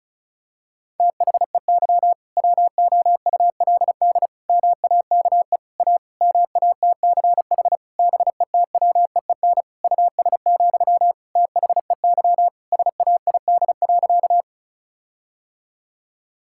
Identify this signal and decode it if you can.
Morse code, heard as a keyed tone, about 36 wpm, 700 Hz. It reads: THEY WOULD MAKE A MATCH BETWEEN US, THEY SAID.
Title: Sentences from Top 1600 Words 35wpm